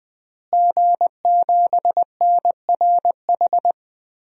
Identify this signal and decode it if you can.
Morse code, heard as a keyed tone, about 20 wpm, 700 Hz. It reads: G7NRH
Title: DXCC-100 Call Signs - 5 Characters - Repeated 3x - 20wpm